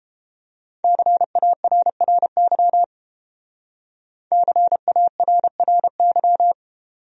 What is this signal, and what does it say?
Morse code, heard as a keyed tone, about 30 wpm, 700 Hz. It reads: CARRY CARRY